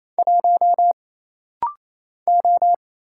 Morse code, heard as a keyed tone, about 27 wpm, 700 Hz. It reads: 1 O
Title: Single Letter-Number - Rapid-Fire 28wpm